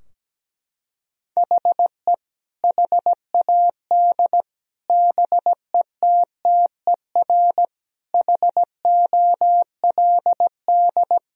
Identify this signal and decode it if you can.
Morse code, heard as a keyed tone, about 17 wpm, 700 Hz. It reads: HE HAD BETTER HOLD
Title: Sentences from Top 1000 Words - Review Part 2 - 17wpm